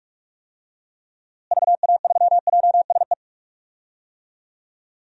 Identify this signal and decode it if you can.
Morse code, heard as a keyed tone, about 46 wpm, 700 Hz. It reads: VA3JSE
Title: DXCC-100 Call Signs - 6 Characters - Repeated 3x - 45wpm